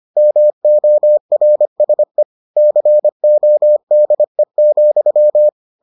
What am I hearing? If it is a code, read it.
Morse code, heard as a keyed tone, about 25 wpm, 600 Hz. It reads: MORSE CODE,